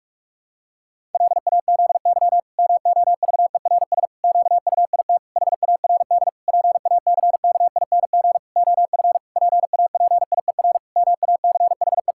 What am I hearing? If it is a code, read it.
Morse code, heard as a keyed tone, about 41 wpm, 700 Hz. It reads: LAZY MOVERS QUIT HARD PACKING OF PAPIER MACHE
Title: Pangram Challenge v2 - 45wpm